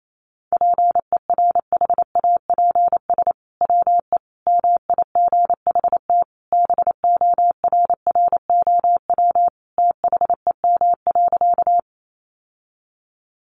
Morse code, heard as a keyed tone, about 28 wpm, 700 Hz. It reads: PERHAPS WE MIGHT BORROW THEM.
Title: Sentences from Top 1500 Words 28wpm